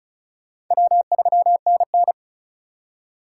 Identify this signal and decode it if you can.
Morse code, heard as a keyed tone, about 42 wpm, 700 Hz. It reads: W3NN
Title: Speed Racing - USA 1x2 Call Signs 35wpm